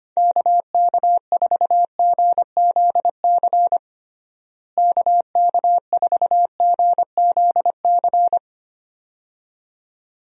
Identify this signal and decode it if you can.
Morse code, heard as a keyed tone, about 25 wpm, 700 Hz. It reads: KK4GZC KK4GZC
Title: USA Call Signs - 6 Characters - Repeated 2x - 25wpm